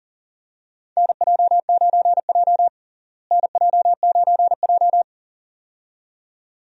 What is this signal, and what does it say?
Morse code, heard as a keyed tone, about 40 wpm, 700 Hz. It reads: NJ9J NJ9J
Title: USA Call Signs - 4 Characters - Repeated 2x - 40wpm